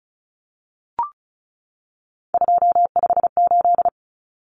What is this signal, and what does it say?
Morse code, heard as a keyed tone, about 34 wpm, 700 Hz. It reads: 258
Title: Numbers - 3-Digit 35wpm